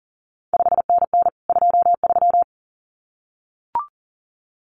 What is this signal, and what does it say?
Morse code, heard as a keyed tone, about 41 wpm, 700 Hz. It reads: 5NN 23
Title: CQ WW DX Contest Exchange Encore - 40wpm